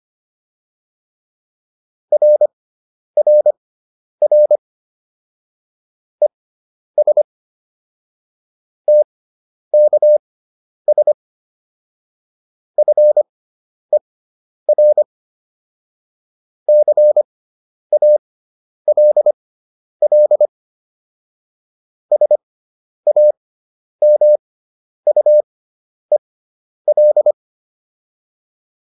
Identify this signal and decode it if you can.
Morse code, heard as a keyed tone, about 25 wpm, spaced as for 10 wpm, 600 Hz. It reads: RRR ES TKS FER CALL SAMUEL
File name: QSO304 10